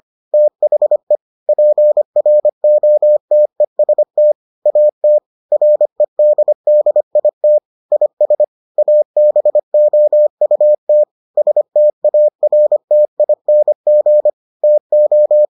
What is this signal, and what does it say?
Morse code, heard as a keyed tone, about 25 wpm, 600 Hz. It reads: THE PROTEST AT REDDIT IS ABOUT STARTING TO